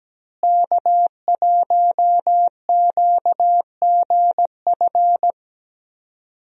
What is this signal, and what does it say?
Morse code, heard as a keyed tone, about 17 wpm, 700 Hz. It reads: K1QGF
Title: USA Call Signs - 5 Characters - Repeated 3x - 17wpm